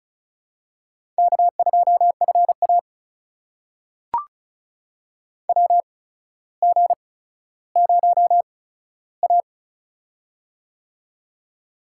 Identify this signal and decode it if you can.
Morse code, heard as a keyed tone, about 36 wpm, spaced as for 15 wpm, 700 Hz. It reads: K2FA WG0A
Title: USA Call Signs - 4 Characters - 8x Character Spacing - 35wpm